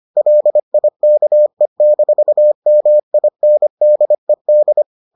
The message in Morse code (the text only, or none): LIKE-MINDED